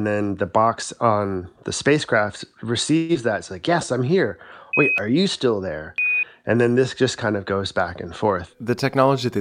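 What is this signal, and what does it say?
Quindar tones, with speech. It is heard from 0.6 to 8.8 s.
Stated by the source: Quindar tones: beeps heard behind the voices of mission control and astronauts on NASA space missions.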